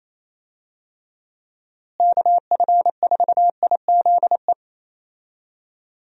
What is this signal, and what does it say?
Morse code, heard as a keyed tone, about 28 wpm, 700 Hz. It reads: KF4IZE
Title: USA Call Signs - 6 Characters - Repeated 2x - 28wpm